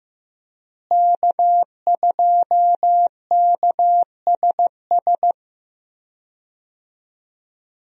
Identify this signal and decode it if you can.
Morse code, heard as a keyed tone, about 15 wpm, 700 Hz. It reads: K2KSS
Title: USA Call Signs - 5 Characters - Repeated 2x - 15wpm